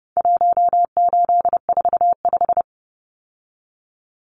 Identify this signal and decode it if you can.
Morse code, heard as a keyed tone, about 30 wpm, 700 Hz. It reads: 1845
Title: Numbers - 4-Digit 30wpm